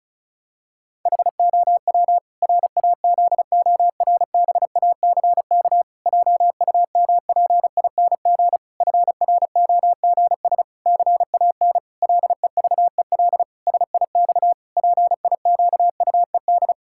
Morse code, heard as a keyed tone, about 36 wpm, 700 Hz. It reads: HOW RAZORBACK JUMPING FROGS CAN LEVEL SIX PIQUED
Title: Pangram Challenge v2 - 35wpm